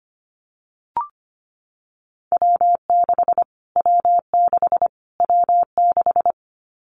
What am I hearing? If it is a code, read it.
Morse code, heard as a keyed tone, about 25 wpm, 700 Hz. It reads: W6 W6 W6
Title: Callsign Prefixes for Top 25 DXCC - Repeated 3x - 25wpm